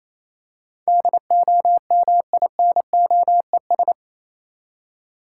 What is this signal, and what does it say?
Morse code, heard as a keyed tone, about 28 wpm, 700 Hz. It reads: DOMINOES